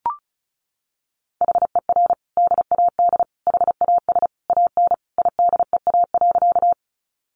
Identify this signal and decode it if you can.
Morse code, heard as a keyed tone, about 34 wpm, 700 Hz. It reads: HER DAD HAS AN IDEA.
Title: Easy Phrases 35wpm